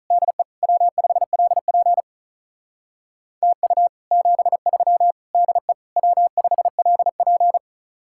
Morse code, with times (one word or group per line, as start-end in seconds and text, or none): 0.1-0.4 DE
0.6-2.0 W5LP
3.4-3.9 TU
4.1-5.1 73
5.3-5.7 DE
6.0-7.6 W5LP